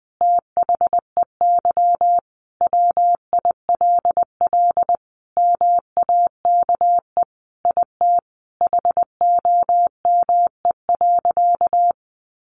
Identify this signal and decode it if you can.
Morse code, heard as a keyed tone, about 19 wpm, 700 Hz. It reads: THEY WILL MAKE IT HOME.
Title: Sentences from Top 1500 Words 20wpm